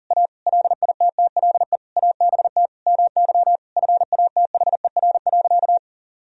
Morse code, heard as a keyed tone, about 40 wpm, 700 Hz. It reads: A LITTLE ABT MY FATHER.
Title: Sentences from Top 400 Words and Abbreviations 40wpm